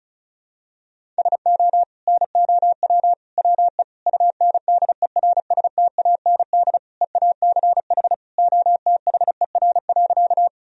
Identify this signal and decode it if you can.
Morse code, heard as a keyed tone, about 35 wpm, 700 Hz. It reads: SO NOW WE UNDERSTAND EACH OTHER.
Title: Sentences from Top 1000 Words - Review Part 1 - 35wpm